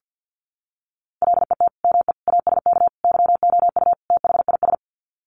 Morse code, heard as a keyed tone, about 50 wpm, 700 Hz. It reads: LET ME ASK YOU THIS
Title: Sets of 5 Words 50wpm